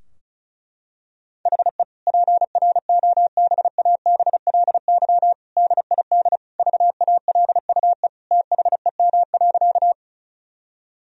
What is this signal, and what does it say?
Morse code, heard as a keyed tone, about 34 wpm, 700 Hz. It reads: HE PROBABLY DID VALUE THEM.
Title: Sentences from Top 1000 Words - Review Part 4 - 35wpm